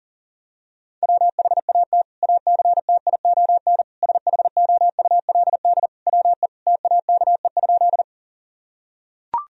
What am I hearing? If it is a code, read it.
Morse code, heard as a keyed tone, about 40 wpm, 700 Hz. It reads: WHAT ACTION SHOULD WE TAKE?
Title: Sentences from Top 1000 Words - Review Part 4 - 40wpm